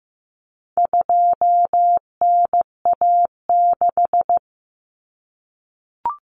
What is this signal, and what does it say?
Morse code, heard as a keyed tone, about 15 wpm, 700 Hz. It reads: 2NA6
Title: Common Cut Numbers - 4-Digit 15wpm